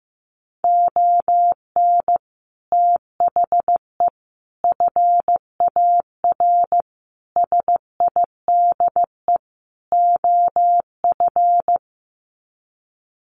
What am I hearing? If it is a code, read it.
Morse code, heard as a keyed tone, about 14 wpm, 700 Hz. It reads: ON THE FAR SIDE OF
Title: Sets of 5 Words 15wpm